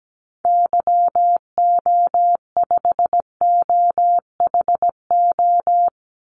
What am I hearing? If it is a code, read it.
Morse code, heard as a keyed tone, about 17 wpm, 700 Hz. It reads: YO5OHO
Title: Call Signs Repeated 3x - 17wpm